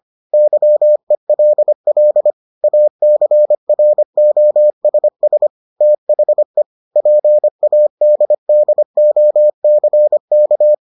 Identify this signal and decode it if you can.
Morse code, heard as a keyed tone, about 25 wpm, 600 Hz. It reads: YELL ACROSS THE PADDOCK